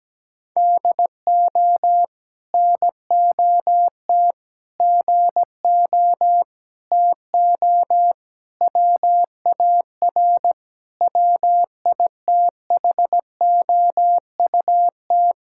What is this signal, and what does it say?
Morse code, heard as a keyed tone, about 17 wpm, 700 Hz. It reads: DO NOT GO TO WAR WITHOUT
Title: Sentences from Top 1000 Words - Review Part 5 - 17wpm